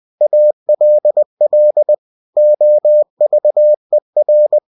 Morse code, heard as a keyed tone, about 20 wpm, 600 Hz. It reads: ALL OVER
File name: cw-books / winnie-the-pooh / winnie-20wpm-page-0049